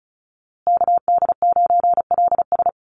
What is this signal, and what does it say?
Morse code, heard as a keyed tone, about 34 wpm, 700 Hz. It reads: KD9LS
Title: USA Call Signs Repeated 2x 35wpm